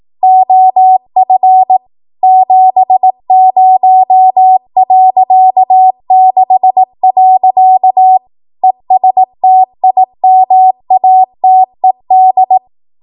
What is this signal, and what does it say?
Morse code, heard as a keyed tone, about 18 wpm, 750 Hz. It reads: OF 70.6. ESTIMATED
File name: Archive / 18 WPM / 18WPM